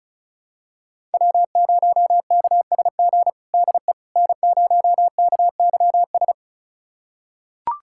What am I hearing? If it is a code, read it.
Morse code, heard as a keyed tone, about 36 wpm, 700 Hz. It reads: W0KSG DE N0KYS
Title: USA Callsign DE Callsign - 5 Characters - 35wpm